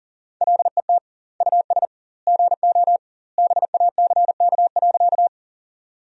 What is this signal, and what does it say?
Morse code, heard as a keyed tone, about 40 wpm, 700 Hz. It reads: LET US GO BACK.
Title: 2x Word-Spacing - Sentences from Top 300 Words - 40wpm